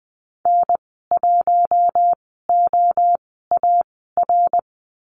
Morse code, heard as a keyed tone, about 20 wpm, spaced as for 15 wpm, 700 Hz. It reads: N1OAR
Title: 055 Lesson 10g - Letter r - Call Signs 20wpm